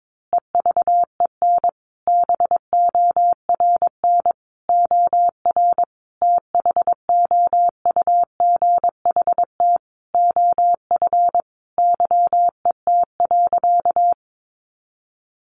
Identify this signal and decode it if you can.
Morse code, heard as a keyed tone, about 22 wpm, 700 Hz. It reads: EVEN BORN OR THOUGHT OF YET.
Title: A Sound Of Thunder 22wpm